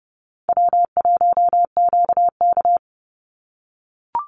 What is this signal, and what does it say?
Morse code, heard as a keyed tone, about 30 wpm, 700 Hz. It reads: W1QK
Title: Easy Call Signs - 30wpm